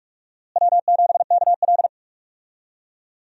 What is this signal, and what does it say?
Morse code, heard as a keyed tone, about 41 wpm, 700 Hz. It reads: W7KL
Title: USA Call Signs - 4 Characters - Repeated 2x - 45wpm